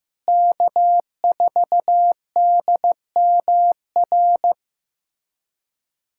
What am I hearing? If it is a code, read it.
Morse code, heard as a keyed tone, about 15 wpm, 700 Hz. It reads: K4DMR
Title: USA Callsign DE Callsign - 5 Characters - 15wpm